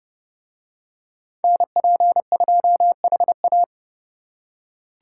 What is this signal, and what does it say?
Morse code, heard as a keyed tone, about 30 wpm, 700 Hz. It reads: NP2HA